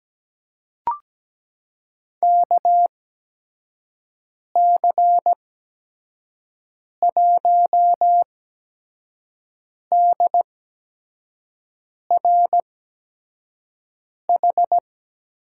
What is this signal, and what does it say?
Morse code, heard as a keyed tone, about 17 wpm, spaced as for 5 wpm, 700 Hz. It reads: KC1DRH
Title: USA Call Signs - 8x Character Spacing - 17wpm